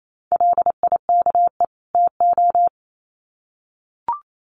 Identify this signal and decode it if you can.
Morse code, heard as a keyed tone, about 28 wpm, 700 Hz. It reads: LIKE TO
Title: Sets of 2 Words 28wpm